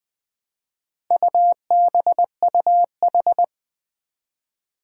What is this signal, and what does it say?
Morse code, heard as a keyed tone, about 21 wpm, 700 Hz. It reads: UBUH
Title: Four Challenging Letter-Number Combinations 20wpm